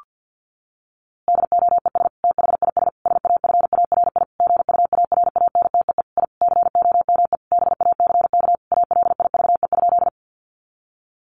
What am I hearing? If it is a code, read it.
Morse code, heard as a keyed tone, about 50 wpm, 700 Hz. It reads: DOES THIS SAFARI GUARANTEE I COME BACK ALIVE?
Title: A Sound Of Thunder 50wpm